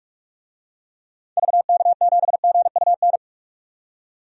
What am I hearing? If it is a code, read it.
Morse code, heard as a keyed tone, about 45 wpm, 700 Hz. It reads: VK7GUN
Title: DXCC-100 Call Signs - 6 Characters - 45wpm